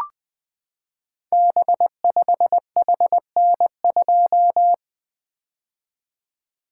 Morse code, heard as a keyed tone, about 20 wpm, 700 Hz. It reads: B5HN2
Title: Code Groups - 5 Letter:Numbers - 20wpm